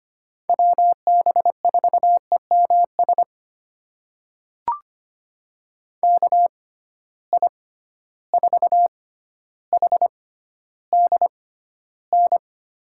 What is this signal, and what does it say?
Morse code, heard as a keyed tone, about 25 wpm, spaced as for 13 wpm, 700 Hz. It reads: WB4EMS KI4HDN
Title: USA Call Signs - 6 Characters - 6x Character Spacing - 25wpm